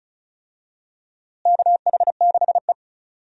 Morse code, heard as a keyed tone, about 36 wpm, 700 Hz. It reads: KH6E
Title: DXCC-100 Call Signs - 4 Characters - Repeated 2x - 35wpm